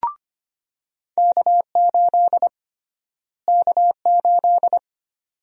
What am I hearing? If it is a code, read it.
Morse code, heard as a keyed tone, about 25 wpm, 700 Hz. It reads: K8 K8
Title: USA Call Sign Prefixes - Repeated 2x - 25wpm